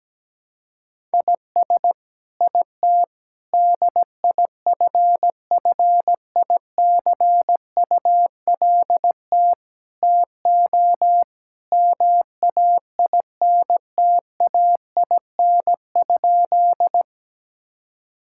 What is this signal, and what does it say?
Morse code, heard as a keyed tone, about 17 wpm, 700 Hz. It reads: IS IT DIFFICULT TO MAINTAIN?
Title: Sentences from Top 1000 Words - Review Part 5 - 17wpm